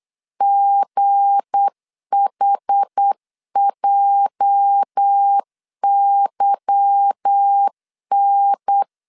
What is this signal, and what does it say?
Morse code, heard as a keyed tone, about 8 wpm, 800 Hz. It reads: GHJYN
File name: LEZIONI / 8 YJ45 / 8 YJ 8wpm 3313